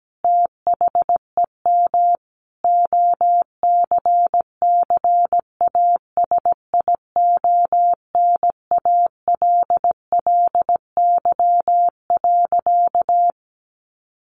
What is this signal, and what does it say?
Morse code, heard as a keyed tone, about 17 wpm, 700 Hz. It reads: THEM OCCASIONALLY.
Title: Sentences from Top 1300 Words 17wpm